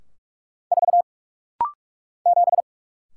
Morse code, heard as a keyed tone, about 44 wpm, 700 Hz. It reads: 4 7
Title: Single Letter-Number - Rapid-Fire 45wpm